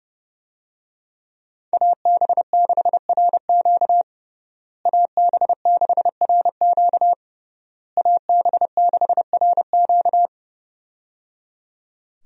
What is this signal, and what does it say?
Morse code, heard as a keyed tone, about 30 wpm, 700 Hz. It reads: AB6RQ AB6RQ AB6RQ
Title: USA Call Signs - 5 Characters - Repeated 3x - 30wpm